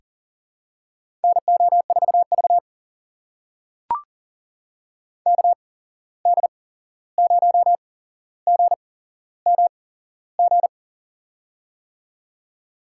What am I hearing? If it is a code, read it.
Morse code, heard as a keyed tone, about 40 wpm, spaced as for 15 wpm, 700 Hz. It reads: NO4V KD0GMG
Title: USA Call Signs - 8x Character Spacing - 40wpm